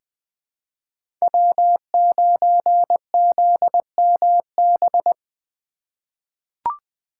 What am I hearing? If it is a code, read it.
Morse code, heard as a keyed tone, about 20 wpm, 700 Hz. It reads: W9ZMB
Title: USA Call Signs - 5 Characters - 4x Character Spacing - 20wpm